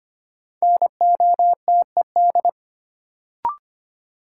Morse code, heard as a keyed tone, about 25 wpm, 700 Hz. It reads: NOTED
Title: Sets of 5 Words - New Word Drill - 25wpm